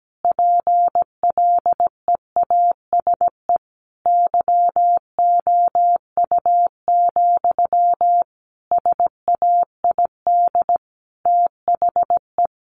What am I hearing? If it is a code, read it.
Morse code, heard as a keyed tone, about 17 wpm, 700 Hz. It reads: PLEASE YOU, SAID THE
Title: Sentences from Top 1500 Words 17wpm